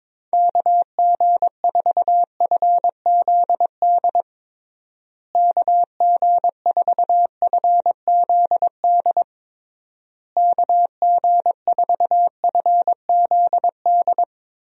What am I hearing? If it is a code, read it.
Morse code, heard as a keyed tone, about 21 wpm, 700 Hz. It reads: KG4FZD KG4FZD KG4FZD
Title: USA Call Signs - 6 Characters - Repeated 3x - 22wpm